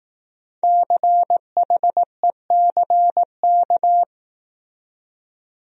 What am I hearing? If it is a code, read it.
Morse code, heard as a keyed tone, about 18 wpm, 700 Hz. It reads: CHECK
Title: Speed-Racing Top 500 Words 15wpm